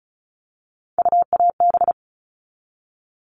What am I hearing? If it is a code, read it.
Morse code, heard as a keyed tone, about 34 wpm, 700 Hz. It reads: UAB
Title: All Cut Numbers - 3-Digit 35wpm